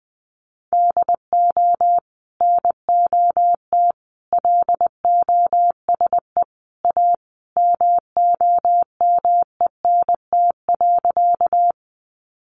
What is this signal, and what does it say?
Morse code, heard as a keyed tone, about 20 wpm, 700 Hz. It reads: DO NOT LOSE A MOMENT.
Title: Sentences from Top 1100 Words 20wpm